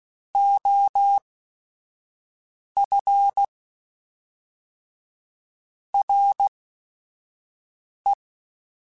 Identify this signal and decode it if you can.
Morse code, heard as a keyed tone, about 16 wpm, spaced as for 5 wpm, 800 Hz. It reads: OF RE
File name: Technical Subjects / Technical Subjects-mp3s / Sw Pwr Sup @5